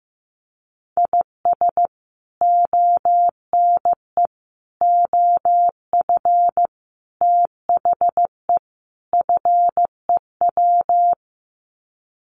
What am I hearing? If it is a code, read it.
Morse code, heard as a keyed tone, about 15 wpm, 700 Hz. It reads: IS ONE OF THE FEW
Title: Sets of 5 Words 15wpm